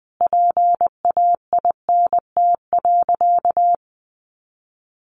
Morse code, heard as a keyed tone, about 20 wpm, 700 Hz. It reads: PAINT.
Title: A Sound Of Thunder 20wpm